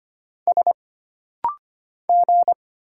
Morse code, heard as a keyed tone, about 25 wpm, 700 Hz. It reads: S G
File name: Single Character (Letter:Number:Punctuation:Prosign) - Rapid-Fire - 25wpm